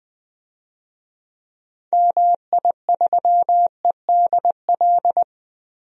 Morse code, heard as a keyed tone, about 20 wpm, 700 Hz. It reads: MI3EDL
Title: DXCC-100 Call Signs - 6 Characters - 20wpm